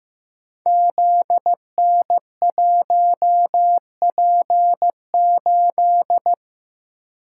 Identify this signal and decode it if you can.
Morse code, heard as a keyed tone, about 15 wpm, 700 Hz. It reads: ZN1P8
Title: Code Groups - 5 Letter:Numbers - 15wpm